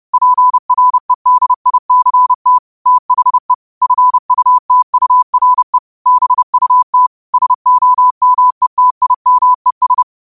Morse code, heard as a keyed tone, about 30 wpm, 1000 Hz. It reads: PREDICT THE FUTURE BUT SOMETIMES